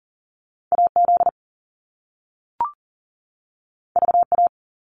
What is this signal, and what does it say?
Morse code, heard as a keyed tone, about 39 wpm, 700 Hz. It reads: AZ VA
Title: US States 40wpm